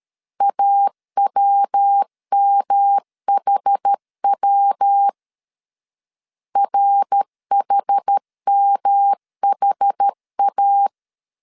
Morse code, heard as a keyed tone, about 13 wpm, 800 Hz. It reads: AWMHW RHMHA
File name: CW / DET / 1 3 1500C S B 12wpm 2828